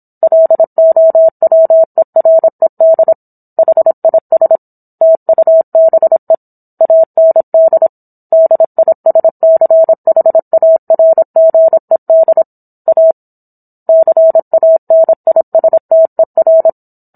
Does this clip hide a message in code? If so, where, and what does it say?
Yes: Morse code from 0.2 to 16.7 s, heard as a keyed tone, about 27 wpm, 650 Hz. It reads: LOWERED HIS TUBE AND DISCHARGED A CANISTER